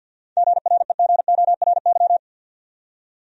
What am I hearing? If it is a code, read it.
Morse code, heard as a keyed tone, about 47 wpm, 700 Hz. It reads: GREGORY
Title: USA Common Names - 50wpm